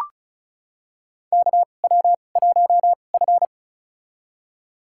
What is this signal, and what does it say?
Morse code, heard as a keyed tone, about 36 wpm, spaced as for 25 wpm, 700 Hz. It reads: KW1F
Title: USA Call Signs - 4 Characters - 2x Character Spacing - 35wpm